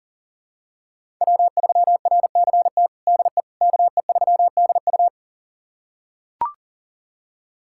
Morse code, heard as a keyed tone, about 37 wpm, 700 Hz. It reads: W3RCT DE KE3DU
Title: USA Callsign DE Callsign - 5 Characters - 40wpm